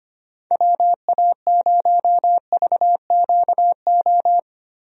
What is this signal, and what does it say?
Morse code, heard as a keyed tone, about 25 wpm, 700 Hz. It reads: WA0VQO
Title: USA Call Signs - 6 Characters - Repeated 3x - 25wpm